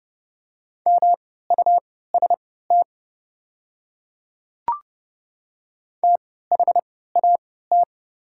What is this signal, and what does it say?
Morse code, heard as a keyed tone, about 30 wpm, spaced as for 17 wpm, 700 Hz. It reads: MUST THAT
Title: 3x Character Spacing - Top 300 Words - 30wpm